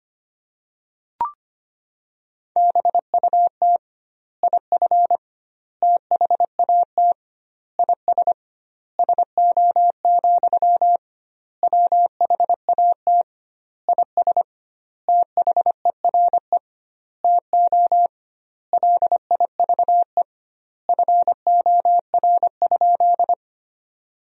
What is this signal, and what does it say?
Morse code, heard as a keyed tone, about 25 wpm, 700 Hz. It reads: BUT IF THAT IS SO, WHAT IS THERE TO LIVE FOR?
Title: 2x Word-Spacing - Sentences from Top 200 Words - 25wpm